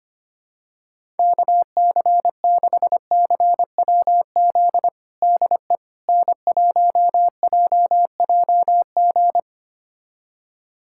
Morse code, heard as a keyed tone, about 25 wpm, 700 Hz. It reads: KC6CWZ DE N1JJG
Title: USA Callsign DE Callsign 25wpm